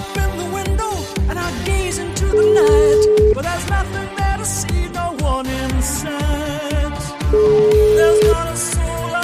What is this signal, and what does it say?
Ringback tone (Europe, 425 Hz), with noise.